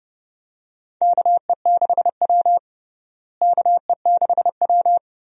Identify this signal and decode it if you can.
Morse code, heard as a keyed tone, about 30 wpm, 700 Hz. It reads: KE6W KE6W
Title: USA Call Signs - 4 Characters - Repeated 2x - 30wpm